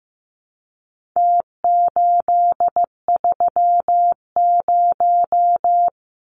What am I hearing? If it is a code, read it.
Morse code, heard as a keyed tone, about 15 wpm, 700 Hz. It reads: T830
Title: Numbers - Contest Style - 4-Digit 15wpm